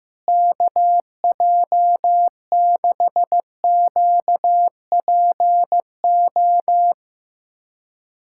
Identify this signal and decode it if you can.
Morse code, heard as a keyed tone, about 15 wpm, 700 Hz. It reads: KJ6QPO
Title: USA Callsign DE Callsign 15wpm